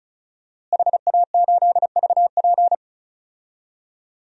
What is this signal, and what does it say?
Morse code, heard as a keyed tone, about 36 wpm, 700 Hz. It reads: HA8VP
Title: DXCC-100 Call Signs - 5 Characters - Repeated 3x - 35wpm